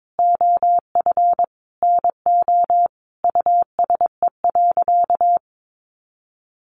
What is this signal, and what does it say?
Morse code, heard as a keyed tone, about 22 wpm, 700 Hz. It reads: OF NO USE.
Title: Sentences from Top 100 Words 22wpm